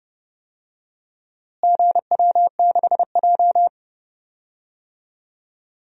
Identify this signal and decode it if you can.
Morse code, heard as a keyed tone, about 30 wpm, 700 Hz. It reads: GW6J
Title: DXCC-100 Call Signs - 4 Characters - Repeated 3x - 30wpm